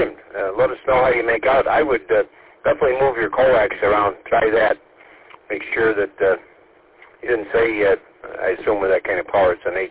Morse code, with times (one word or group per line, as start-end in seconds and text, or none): none